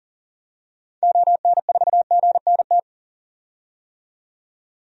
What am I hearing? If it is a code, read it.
Morse code, heard as a keyed tone, about 40 wpm, 700 Hz. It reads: ON4GNT
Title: DXCC-100 Call Signs - 6 Characters - 40wpm